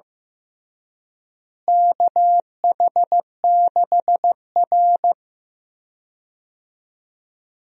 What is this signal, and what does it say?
Morse code, heard as a keyed tone, about 15 wpm, 700 Hz. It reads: KH6R
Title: USA Call Signs - 4 Characters - Repeated 3x - 15wpm